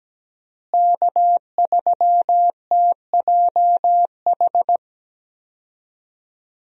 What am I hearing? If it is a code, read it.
Morse code, heard as a keyed tone, about 17 wpm, 700 Hz. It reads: K3TJH
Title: USA Call Signs - 5 Characters - Repeated 3x - 17wpm